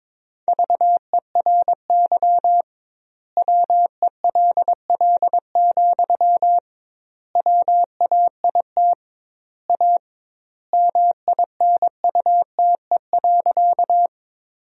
Morse code, heard as a keyed tone, about 22 wpm, 700 Hz. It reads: VERY WELL, WAIT A MINUTE.
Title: 2x Word-Spacing - Sentences from Top 500 Words - 22wpm